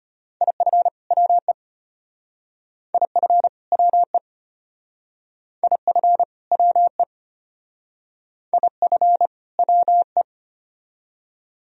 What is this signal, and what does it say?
Morse code, heard as a keyed tone, about 34 wpm, 700 Hz. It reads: IF WE IF WE IF WE IF WE